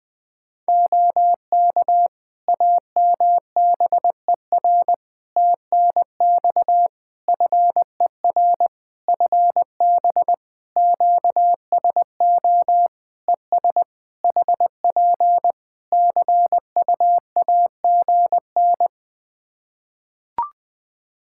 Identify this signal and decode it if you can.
Morse code, heard as a keyed tone, about 20 wpm, 700 Hz. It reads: OK AMBER TNX FER FB QSO ES HP CUAGN
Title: Practice QSO - Exchange 4 - Ending - 20wpm